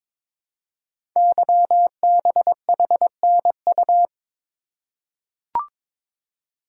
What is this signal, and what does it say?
Morse code, heard as a keyed tone, about 22 wpm, 700 Hz. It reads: YBHNU
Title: Code Groups - 5 Letter:Numbers - 22wpm